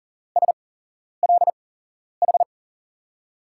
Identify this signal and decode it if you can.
Morse code, heard as a keyed tone, about 40 wpm, spaced as for 11 wpm, 700 Hz. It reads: SLH
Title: USA Call Signs - 8x Character Spacing - 40wpm